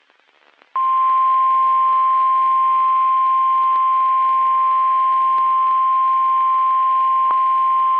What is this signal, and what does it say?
Emergency Alert System (SAME) tones (the 1050 Hz attention tone). It is heard from 0.7 to 7.3 s.
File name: eas messages / EAS WXM31 Elmira